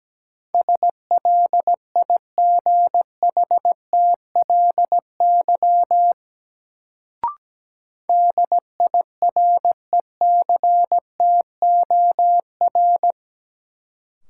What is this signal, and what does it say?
Morse code, heard as a keyed tone, about 17 wpm, 700 Hz. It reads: SLIGHTLY DIRECTOR
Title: Sentences from Top 1300 Words - New Word Drill Alt - 17wpm